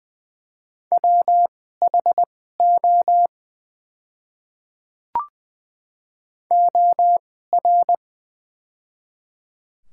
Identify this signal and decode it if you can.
Morse code, heard as a keyed tone, about 20 wpm, spaced as for 14 wpm, 700 Hz. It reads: WHO OR
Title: 2x Character Spacing - Top 100 Words - 20wpm